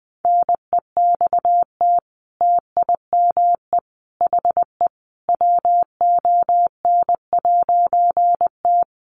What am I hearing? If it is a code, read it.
Morse code, heard as a keyed tone, about 20 wpm, 700 Hz. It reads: NEXT TIME HE WON'T
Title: A Sound Of Thunder 20wpm